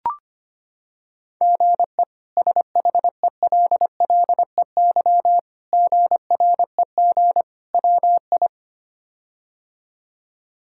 Morse code, heard as a keyed tone, about 25 wpm, 700 Hz. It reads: GE SHELLEY GREG WI
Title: K1USN SST QSO Practice - 25wpm